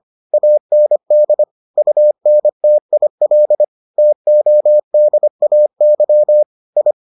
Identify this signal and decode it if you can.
Morse code, heard as a keyed tone, about 25 wpm, 600 Hz. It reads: AND UNTIL TODAY I